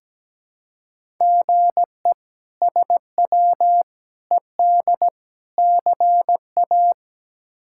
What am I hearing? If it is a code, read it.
Morse code, heard as a keyed tone, about 17 wpm, 700 Hz. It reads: GE SW ED CA
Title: K1USN SST QSO Practice - 17wpm